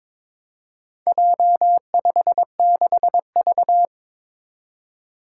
Morse code, heard as a keyed tone, about 22 wpm, 700 Hz. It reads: J56V
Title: Four Challenging Letter-Number Combinations 22wpm